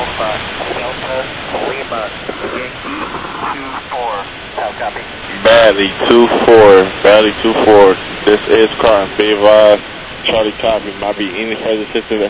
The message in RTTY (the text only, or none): none